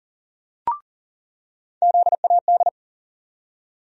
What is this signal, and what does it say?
Morse code, heard as a keyed tone, about 40 wpm, 700 Hz. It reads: ZAD
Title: USA Call Sign Postfixes - 3 Letters 40wpm